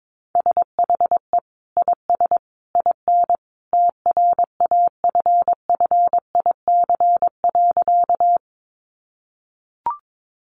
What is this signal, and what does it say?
Morse code, heard as a keyed tone, about 22 wpm, 700 Hz. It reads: SHE IS IN TRAFFIC.
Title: Sentences from Top 1300 Words 22wpm